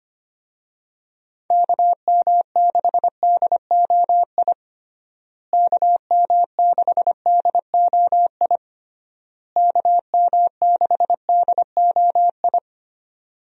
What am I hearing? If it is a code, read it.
Morse code, heard as a keyed tone, about 25 wpm, 700 Hz. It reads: KM6DOI KM6DOI KM6DOI
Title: USA Call Signs - 6 Characters - Repeated 3x - 25wpm